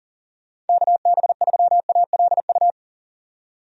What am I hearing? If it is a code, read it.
Morse code, heard as a keyed tone, about 40 wpm, 700 Hz. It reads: KB3ALU
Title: USA Call Signs - 6 Characters - 40wpm